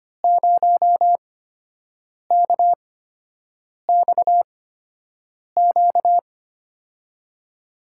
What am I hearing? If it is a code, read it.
Morse code, heard as a keyed tone, about 25 wpm, spaced as for 7 wpm, 700 Hz. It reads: 0KXQ
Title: USA Call Signs - 6 Characters - 8x Character Spacing - 25wpm